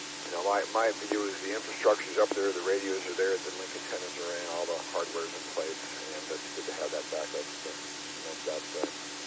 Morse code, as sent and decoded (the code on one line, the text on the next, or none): none